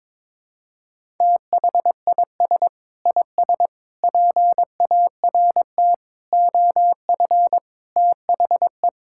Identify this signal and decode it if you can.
Morse code, heard as a keyed tone, about 22 wpm, 700 Hz. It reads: THIS IS PART OF THE
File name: practice / 22wpm / Sentences with 2-Word Phrases - 22wpm